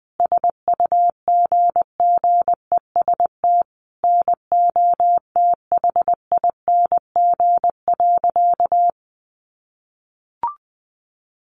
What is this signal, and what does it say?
Morse code, heard as a keyed tone, about 20 wpm, 700 Hz. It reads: SUGGEST NOTHING.
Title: Sentences from Top 1300 Words 20wpm